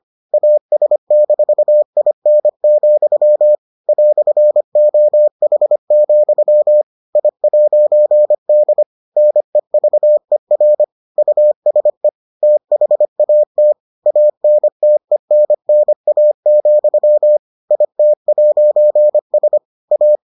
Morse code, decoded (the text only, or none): AS-IN, "OH, I'D NEVER USE THAT ANTENNA, IT'S A